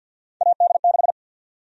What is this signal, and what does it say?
Morse code, heard as a keyed tone, about 53 wpm, 700 Hz. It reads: AD6
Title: USA Call Sign Prefixes - 50wpm